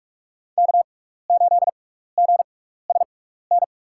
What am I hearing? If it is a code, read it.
Morse code, heard as a keyed tone, about 46 wpm, spaced as for 16 wpm, 700 Hz. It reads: K8GSN
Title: USA Call Signs - 5 Characters - 6x Character Spacing - 45wpm